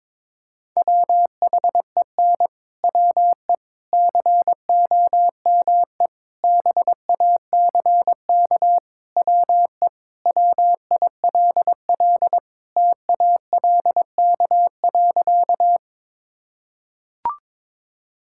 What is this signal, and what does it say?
Morse code, heard as a keyed tone, about 22 wpm, 700 Hz. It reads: WHEN WE COME BACK WE WILL TALK.
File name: practice / 22wpm / Sentences with 4-Word Phrases - 22wpm